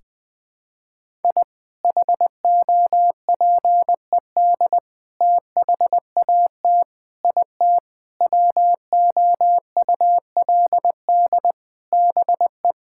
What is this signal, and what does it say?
Morse code, heard as a keyed tone, about 21 wpm, 700 Hz. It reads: I HOPED THAT IT WOULD BE
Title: Sentences with 4-Word Phrases - 20wpm